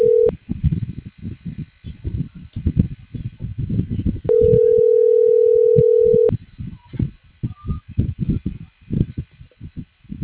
Ringback tone (North America), 440 + 480 Hz.